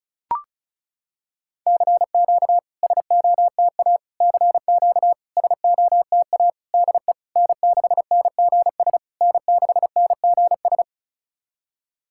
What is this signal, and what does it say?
Morse code, heard as a keyed tone, about 36 wpm, 690 Hz. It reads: CQ SOTA CQ SOTA DE N6NGS N6NGS
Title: SOTA QSO Practice 35wpm